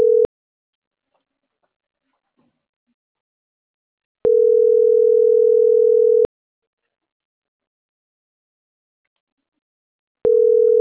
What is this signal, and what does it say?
Ringback tone (North America), 440 + 480 Hz.